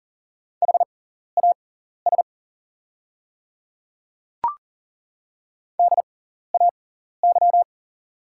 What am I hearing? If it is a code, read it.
Morse code, heard as a keyed tone, about 40 wpm, spaced as for 14 wpm, 700 Hz. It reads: HAS DAY
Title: 6x Character Spacing - Top 100 Words - 40wpm